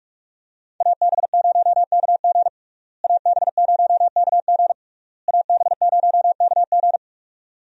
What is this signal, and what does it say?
Morse code, heard as a keyed tone, about 41 wpm, 700 Hz. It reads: AB0KG AB0KG AB0KG
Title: USA Call Signs - 5 Characters - Repeated 3x - 45wpm